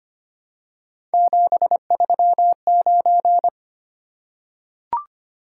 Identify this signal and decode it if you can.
Morse code, heard as a keyed tone, about 25 wpm, 700 Hz. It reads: 739
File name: Three Letter-Number Combinations - 25wpm